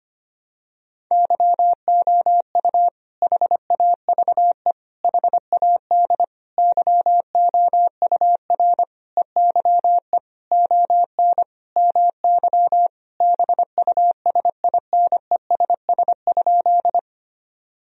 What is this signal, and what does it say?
Morse code, heard as a keyed tone, about 25 wpm, 700 Hz. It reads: YOU HAVE HAD YOUR EYE ON MY BUSINESS?
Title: Sentences from Top 1000 Words - Review Part 1 - 25wpm